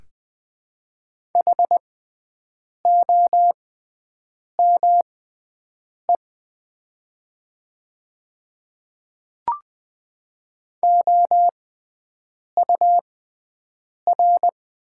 Morse code, heard as a keyed tone, about 20 wpm, spaced as for 7 wpm, 700 Hz. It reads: HOME OUR
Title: 6x Character Spacing - Top 200 Words - 20wpm